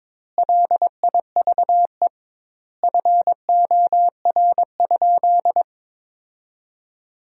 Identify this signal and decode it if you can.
Morse code, heard as a keyed tone, about 22 wpm, 700 Hz. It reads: LIVE FOR?
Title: 2x Word-Spacing - Sentences from Top 200 Words - 22wpm